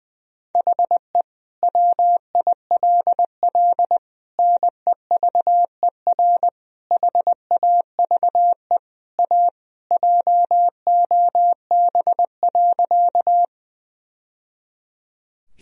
Morse code, heard as a keyed tone, about 20 wpm, 700 Hz. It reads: HE WILL NEVER HAVE A JOB.
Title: Sentences from Top 1000 Words - Review Part 3 - 20wpm